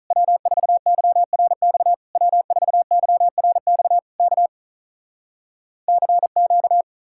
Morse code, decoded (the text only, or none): W4YRX W4YRX K CQ